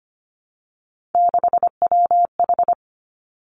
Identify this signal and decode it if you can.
Morse code, heard as a keyed tone, about 25 wpm, 700 Hz. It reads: 6WH
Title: Three Challenging Letter-Number Combinations 25wpm